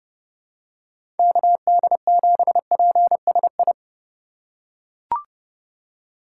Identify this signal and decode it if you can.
Morse code, heard as a keyed tone, about 30 wpm, 700 Hz. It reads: KD7PSI
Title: USA Call Signs - 4x Character Spacing - 30wpm